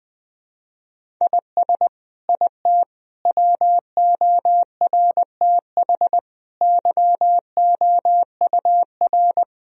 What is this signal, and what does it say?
Morse code, heard as a keyed tone, about 20 wpm, 700 Hz. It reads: IS IT WORTH YOUR
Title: Sentences from Top 1000 Words - Review Part 1 - 20wpm